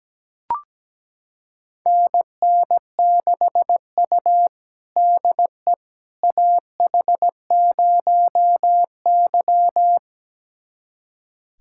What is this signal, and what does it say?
Morse code, heard as a keyed tone, about 17 wpm, 700 Hz. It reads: NN6U DE AH0Y
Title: USA Callsign DE Callsign - 4 Characters - 17wpm